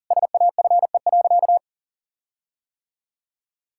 Morse code, heard as a keyed tone, about 37 wpm, 700 Hz. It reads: SAFE.
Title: Sentences from Top 1600 Words 40wpm